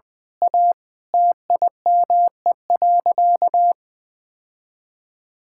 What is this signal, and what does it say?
Morse code, heard as a keyed tone, about 20 wpm, 700 Hz. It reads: A TIME.
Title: Sentences from Top 1000 Words - Review Part 6 - 20wpm